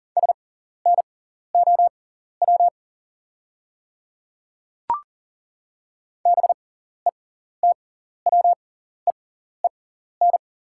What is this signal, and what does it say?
Morse code, heard as a keyed tone, about 40 wpm, spaced as for 14 wpm, 690 Hz. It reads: SNOW BETWEEN
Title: 6x Character Spacing - Top 500 Words - 40wpm